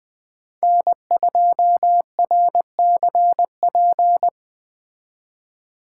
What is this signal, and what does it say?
Morse code, heard as a keyed tone, about 21 wpm, 700 Hz. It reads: N2RCP
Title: Speed Racing - USA 1x3 Call Signs 17wpm